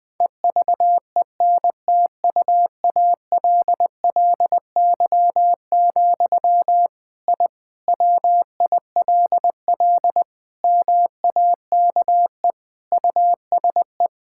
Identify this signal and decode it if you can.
Morse code, heard as a keyed tone, about 21 wpm, 700 Hz. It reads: EVENTUALLY, I WILL MAKE USE
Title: Sentences from Top 1000 Words - Review Part 6 - 20wpm